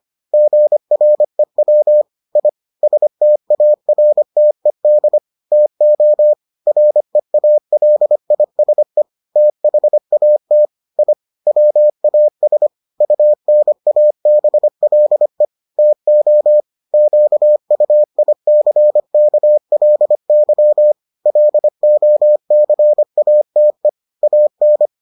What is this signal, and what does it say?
Morse code, heard as a keyed tone, about 25 wpm, 600 Hz. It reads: GREW I STARTED TO REALISE THAT I WAS UNABLE TO QUICKLY LOCATE AN